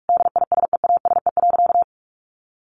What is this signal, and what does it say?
Morse code, heard as a keyed tone, about 44 wpm, 700 Hz. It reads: DISEASE.
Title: Sentences from Top 1500 Words 45wpm